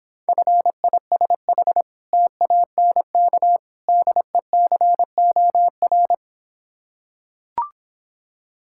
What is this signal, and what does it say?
Morse code, heard as a keyed tone, about 26 wpm, 690 Hz. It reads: FISH TANK DECOR